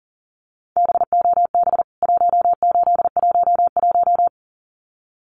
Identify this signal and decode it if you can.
Morse code, heard as a keyed tone, about 40 wpm, 700 Hz. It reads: BOB 1811
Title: CWT Exchange 40wpm